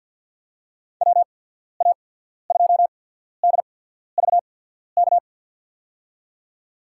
Morse code, heard as a keyed tone, about 53 wpm, spaced as for 14 wpm, 700 Hz. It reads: WA2DVK
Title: USA Call Signs - 6 Characters - 8x Character Spacing - 50wpm